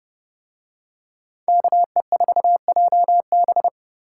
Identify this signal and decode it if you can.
Morse code, heard as a keyed tone, about 28 wpm, 700 Hz. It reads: KE4JB